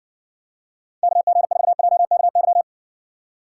Morse code, heard as a keyed tone, about 60 wpm, 700 Hz. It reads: KG4JKY